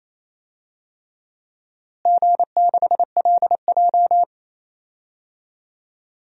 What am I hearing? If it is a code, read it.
Morse code, heard as a keyed tone, about 28 wpm, 700 Hz. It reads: G6LJ
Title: DXCC-100 Call Signs - 4 Characters - Repeated 3x - 28wpm